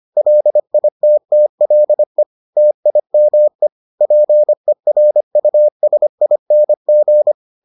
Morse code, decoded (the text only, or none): LITTLE TIME PERUSING